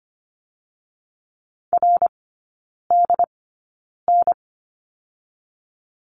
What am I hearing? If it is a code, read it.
Morse code, heard as a keyed tone, about 25 wpm, spaced as for 9 wpm, 690 Hz. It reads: RDN
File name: Lesson 15e - Number 5 - Three Characters 25wpm